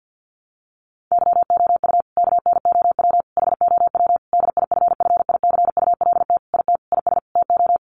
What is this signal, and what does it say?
Morse code, heard as a keyed tone, about 49 wpm, 700 Hz. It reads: YOU KNOW HOW DIFFICULT IT IS TO
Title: Sentences from Top 1100 Words 50wpm